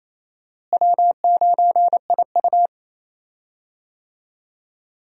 Morse code, heard as a keyed tone, about 28 wpm, 700 Hz. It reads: W9IU
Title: USA Call Signs - 4 Characters - Repeated 2x - 28wpm